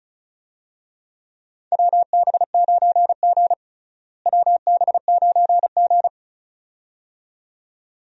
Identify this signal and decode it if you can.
Morse code, heard as a keyed tone, about 36 wpm, 700 Hz. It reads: WB9G WB9G
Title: USA Call Signs - 4 Characters - Repeated 2x - 35wpm